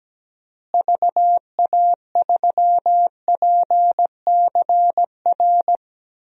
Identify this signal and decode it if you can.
Morse code, heard as a keyed tone, about 17 wpm, 700 Hz. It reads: VA3PCR
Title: DXCC-100 Call Signs - 6 Characters - Repeated 3x - 17wpm